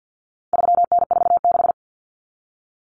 Morse code, heard as a keyed tone, about 50 wpm, 700 Hz. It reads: 3N46
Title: Common Cut Numbers - 4-Digit 50wpm